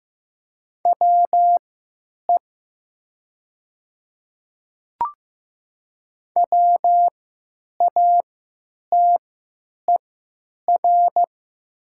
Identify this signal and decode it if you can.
Morse code, heard as a keyed tone, about 15 wpm, spaced as for 8 wpm, 700 Hz. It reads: WE WATER
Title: 3x Character Spacing - Top 100 Words - 15wpm